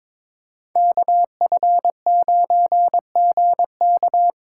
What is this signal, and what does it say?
Morse code, heard as a keyed tone, about 26 wpm, 700 Hz. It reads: KF9GK
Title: Speed Racing - USA 2x2 Call Signs 22wpm